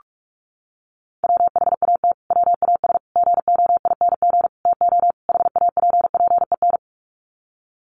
Morse code, heard as a keyed tone, about 44 wpm, 700 Hz. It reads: WHAT WAS GOING TO HAPPEN
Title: Sets of 5 Words Encore 45wpm